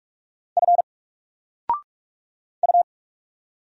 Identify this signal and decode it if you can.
Morse code, heard as a keyed tone, about 44 wpm, 700 Hz. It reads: F U
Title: Single Character (Letter:Number:Punctuation:Prosign) 45wpm